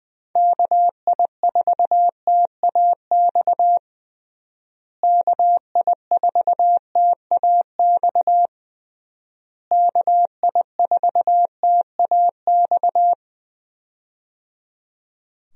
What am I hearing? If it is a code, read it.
Morse code, heard as a keyed tone, about 20 wpm, 700 Hz. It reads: KI4TAX KI4TAX KI4TAX
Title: USA Call Signs - 6 Characters - Repeated 3x - 20wpm